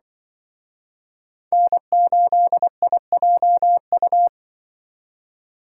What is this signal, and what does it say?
Morse code, heard as a keyed tone, about 25 wpm, 700 Hz. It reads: N8IJU
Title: Speed Racing - USA 1x3 Call Signs 20wpm